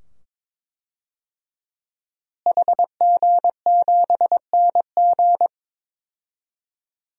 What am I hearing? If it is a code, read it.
Morse code, heard as a keyed tone, about 22 wpm, 700 Hz. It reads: HG7NG
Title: DXCC-100 Call Signs - 5 Characters - Repeated 2x - 22wpm